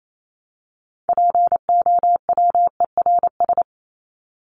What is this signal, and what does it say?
Morse code, heard as a keyed tone, about 28 wpm, 700 Hz. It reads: POWERS
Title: ES:S Suffix - Top 100 Words with Base Words 28wpm